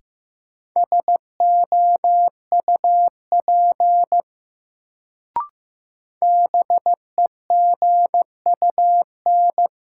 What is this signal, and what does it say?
Morse code, heard as a keyed tone, about 15 wpm, 700 Hz. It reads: SOUP BEGUN
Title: Sentences from Top 1000 Words - Review Part 6 - New Word Drill - 15wpm